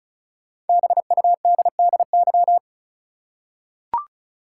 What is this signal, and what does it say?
Morse code, heard as a keyed tone, about 34 wpm, 700 Hz. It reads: BUDDY
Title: Five Letter Words - 35wpm